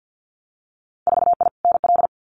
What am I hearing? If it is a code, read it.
Morse code, heard as a keyed tone, about 50 wpm, 700 Hz. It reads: 4I NL
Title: Winter Field Day Exchanges Encore 50wpm